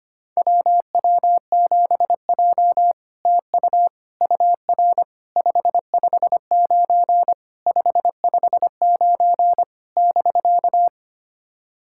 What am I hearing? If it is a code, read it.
Morse code, heard as a keyed tone, about 25 wpm, 690 Hz. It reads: WW7J TU UR 559 559 <BK>
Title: SOTA QSO Practice 25wpm